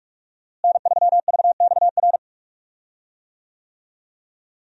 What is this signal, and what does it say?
Morse code, heard as a keyed tone, about 46 wpm, 700 Hz. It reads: N3VXR
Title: USA Call Signs - 5 Characters - Repeated 3x - 45wpm